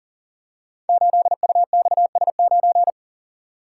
Morse code, heard as a keyed tone, about 40 wpm, 700 Hz. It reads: 8UXS9
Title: Code Groups - 5 Letter:Numbers - 40wpm